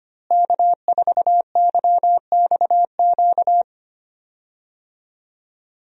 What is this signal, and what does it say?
Morse code, heard as a keyed tone, about 25 wpm, 700 Hz. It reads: K4YXQ
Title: USA Call Signs - 5 Characters - Repeated 3x - 25wpm